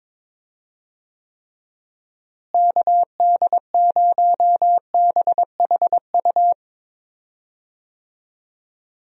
Morse code, heard as a keyed tone, about 21 wpm, 700 Hz. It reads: KD0BHU
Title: USA Call Signs - 6 Characters - Repeated 2x - 22wpm